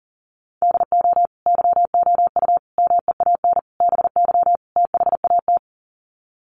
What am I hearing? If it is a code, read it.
Morse code, heard as a keyed tone, about 37 wpm, 700 Hz. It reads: DO YOU MEAN BY THAT
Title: Sets of 5 Words Encore2 40wpm